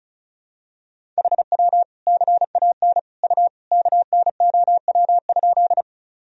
Morse code, heard as a keyed tone, about 36 wpm, 700 Hz. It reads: HW CAN U KNOW?
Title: Sentences from Top 100 Words and Abbreviations 35wpm